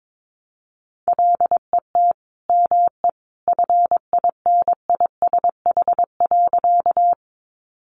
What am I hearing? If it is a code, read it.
Morse code, heard as a keyed tone, about 22 wpm, 700 Hz. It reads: LET ME FINISH.
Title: Sentences from Top 1200 Words 22wpm